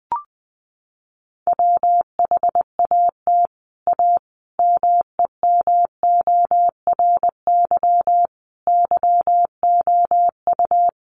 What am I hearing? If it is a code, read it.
Morse code, heard as a keyed tone, about 20 wpm, 700 Hz. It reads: WHAT A MEMORY YOU
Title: Sentences from Top 1300 Words 20wpm